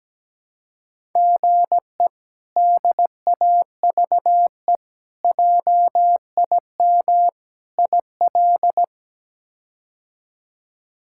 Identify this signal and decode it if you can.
Morse code, heard as a keyed tone, about 17 wpm, 700 Hz. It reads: GE DAVE JIM IL
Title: K1USN SST QSO Practice - 17wpm